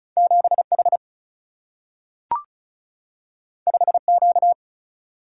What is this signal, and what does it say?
Morse code, heard as a keyed tone, about 36 wpm, 700 Hz. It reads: 7H 5Q
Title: Single-Letter Single-Number 35wpm